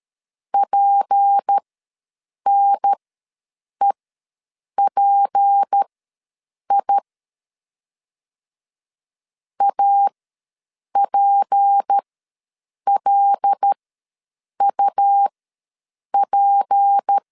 Morse code, heard as a keyed tone, about 13 wpm, spaced as for 7 wpm, 800 Hz. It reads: PNEPI APLUP
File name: LEZIONI / 6 PKX0 / 6 P ws 12wpm 3117